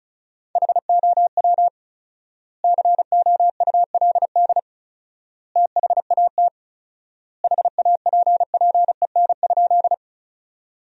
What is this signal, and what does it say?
Morse code, heard as a keyed tone, about 36 wpm, 700 Hz. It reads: HOW COULD THAT HAPPEN?
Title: 4x Word-Spacing - Sentences from Top 400 Words - 35wpm